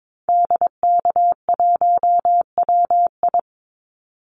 Morse code, heard as a keyed tone, about 22 wpm, 700 Hz. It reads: DK1WI
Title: Call Signs - 22wpm